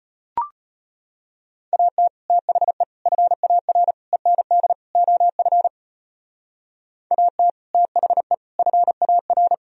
Morse code, heard as a keyed tone, about 31 wpm, 700 Hz. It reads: AT THE FAR END OF AT THE FAR
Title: Speed-Racing - Sets of 5 Words 25wpm